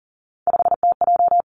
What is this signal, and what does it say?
Morse code, heard as a keyed tone, about 40 wpm, 700 Hz. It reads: 5TJ
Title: Three Challenging Letter-Number Combinations 40wpm